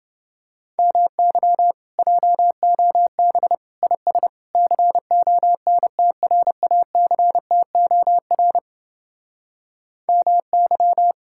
Morse code, heard as a keyed tone, about 29 wpm, 700 Hz. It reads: MY JOB IS CONTRACTOR MY